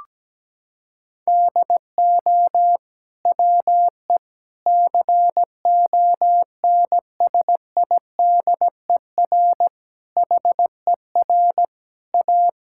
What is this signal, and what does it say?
Morse code, heard as a keyed tone, about 17 wpm, 700 Hz. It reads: DO WE CONSIDER HER A
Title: Sentences from Top 1000 Words - Review Part 5 - 17wpm